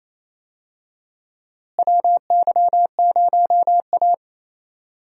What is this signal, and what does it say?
Morse code, heard as a keyed tone, about 28 wpm, 700 Hz. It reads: WY0A
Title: USA Call Signs - 4 Characters - Repeated 2x - 28wpm